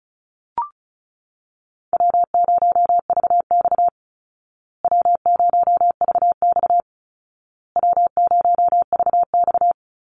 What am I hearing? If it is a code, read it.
Morse code, heard as a keyed tone, about 34 wpm, 700 Hz. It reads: W0VX W0VX W0VX
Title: Easy Call Signs Repeated 3x - 35wpm